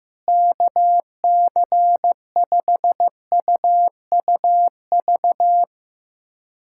Morse code, heard as a keyed tone, about 15 wpm, 700 Hz. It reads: KC5UUV
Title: USA Callsign DE Callsign - 6 Characters - 15wpm